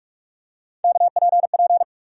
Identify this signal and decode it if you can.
Morse code, heard as a keyed tone, about 43 wpm, 700 Hz. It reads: KPP
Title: USA Call Sign Postfixes - 3 Letters 45wpm